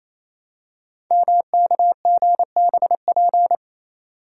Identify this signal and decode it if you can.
Morse code, heard as a keyed tone, about 28 wpm, 700 Hz. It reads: MKGBP
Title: Code Groups - 5 Letters - 28wpm